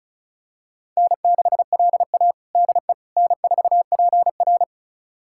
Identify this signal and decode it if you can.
Morse code, heard as a keyed tone, about 36 wpm, 700 Hz. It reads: N6LA DE N4PR
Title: USA Callsign DE Callsign - 4 Characters - 35wpm